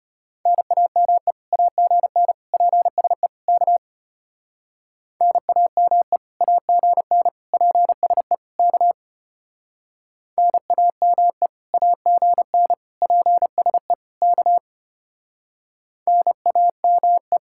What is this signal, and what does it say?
Morse code, heard as a keyed tone, about 32 wpm, 700 Hz. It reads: NAME AGN PSE K NAME AGN PSE K NAME AGN PSE K NAME